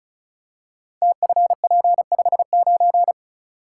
Morse code, heard as a keyed tone, about 36 wpm, 700 Hz. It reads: TFP59
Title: Code Groups - 5 Letter:Numbers - 35wpm